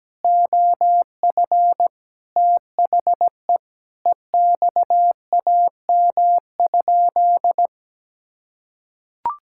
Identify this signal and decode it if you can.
Morse code, heard as a keyed tone, about 17 wpm, 700 Hz. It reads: OF THE EXAM?
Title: Sentences from Top 1000 Words - Review Part 5 - 17wpm